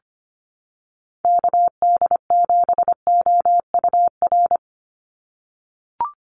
Morse code, heard as a keyed tone, about 25 wpm, 690 Hz. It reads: KD7OUR
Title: USA Call Signs 25wpm